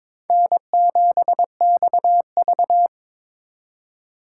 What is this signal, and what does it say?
Morse code, heard as a keyed tone, about 26 wpm, 700 Hz. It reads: N7XV
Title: Speed Racing - USA Call Signs 22wpm